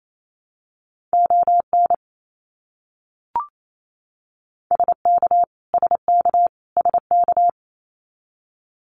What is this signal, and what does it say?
Morse code, heard as a keyed tone, about 28 wpm, 700 Hz. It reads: ON SK SK SK
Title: Canadian Provinces and Territories - Repeated 3x - 28wpm